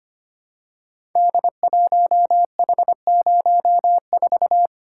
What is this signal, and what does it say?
Morse code, heard as a keyed tone, about 25 wpm, 700 Hz. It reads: D1H04
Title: Code Groups - 5 Letter:Numbers - 25wpm